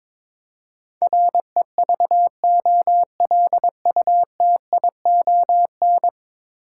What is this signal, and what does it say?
Morse code, heard as a keyed tone, about 22 wpm, 700 Hz. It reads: REVOLUTION
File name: practice / 22wpm / 10 Letter Words - First 3 Letters Given - 22wpm